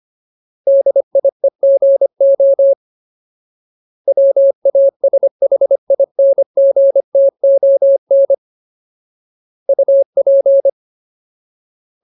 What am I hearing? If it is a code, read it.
Morse code, heard as a keyed tone, about 25 wpm, 550 Hz. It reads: DIEGO WASHINGTON UP
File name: Ultimate Word List / 25WPM Ultimate